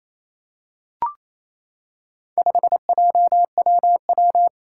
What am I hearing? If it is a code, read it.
Morse code, heard as a keyed tone, about 28 wpm, 700 Hz. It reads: 5JWW
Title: Four Challenging Letter-Number Combinations 28wpm